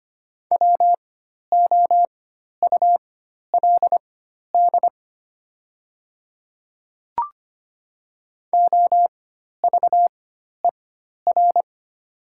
Morse code, heard as a keyed tone, about 25 wpm, spaced as for 12 wpm, 700 Hz. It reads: WOULD OVER
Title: 4x Character Spacing - Top 100 Words - 25wpm